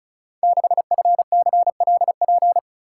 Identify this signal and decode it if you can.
Morse code, heard as a keyed tone, about 36 wpm, 700 Hz. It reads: 6FCLP
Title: Code Groups - 5 Letter:Numbers - 35wpm